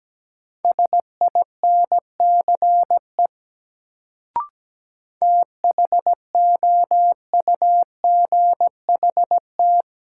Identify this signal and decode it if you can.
Morse code, heard as a keyed tone, about 17 wpm, 700 Hz. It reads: SINCE THOUGHT
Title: Sets of 2 Words - New Word Drill - 17wpm